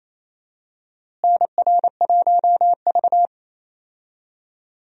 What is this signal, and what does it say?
Morse code, heard as a keyed tone, about 28 wpm, 700 Hz. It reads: NR1V
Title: USA Call Signs - 4 Characters - Repeated 2x - 28wpm